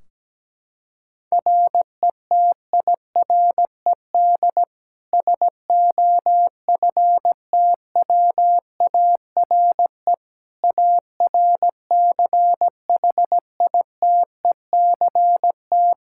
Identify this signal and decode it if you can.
Morse code, heard as a keyed tone, about 17 wpm, 700 Hz. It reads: RETIRED SOFTWARE ARCHITECT
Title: Practice QSO - Exchange 3 - More Info - 17wpm